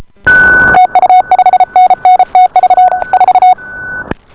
Morse code, heard as a keyed tone, about 33 wpm, 740 Hz. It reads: TU 5NN TV 4